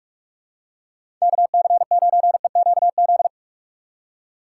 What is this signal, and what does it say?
Morse code, heard as a keyed tone, about 46 wpm, 700 Hz. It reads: KC9EQZ
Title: USA Call Signs - 6 Characters - 2x Character Spacing - 45wpm